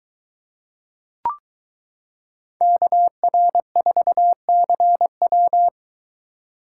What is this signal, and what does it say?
Morse code, heard as a keyed tone, about 23 wpm, 700 Hz. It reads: KR4CW